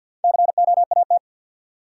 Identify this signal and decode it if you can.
Morse code, heard as a keyed tone, about 53 wpm, 700 Hz. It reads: COAT
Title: Sentences from Top 1600 Words - New Word Drill - 50wpm